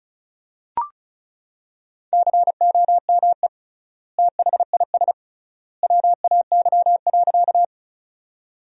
Morse code, heard as a keyed tone, about 36 wpm, 700 Hz. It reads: COME THIS WAY.
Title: 3x Word-Spacing - Sentences from Top 100 Words - 35wpm